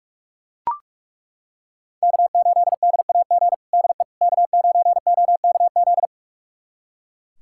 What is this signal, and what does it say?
Morse code, heard as a keyed tone, about 42 wpm, 700 Hz. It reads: K8DAG DE K9OKZ
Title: USA Callsign DE Callsign - 5 Characters - 45wpm